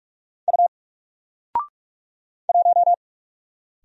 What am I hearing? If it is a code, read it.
Morse code, heard as a keyed tone, about 44 wpm, 700 Hz. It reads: U 1
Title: Single Letter-Number 45wpm